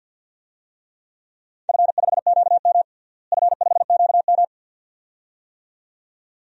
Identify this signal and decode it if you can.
Morse code, heard as a keyed tone, about 52 wpm, 700 Hz. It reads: F5QM F5QM
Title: DXCC-100 Call Signs - 4 Characters - Repeated 2x - 50wpm